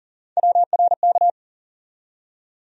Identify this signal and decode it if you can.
Morse code, heard as a keyed tone, about 40 wpm, 700 Hz. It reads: WRK
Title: USA Call Sign Postfixes - 3 Letters 40wpm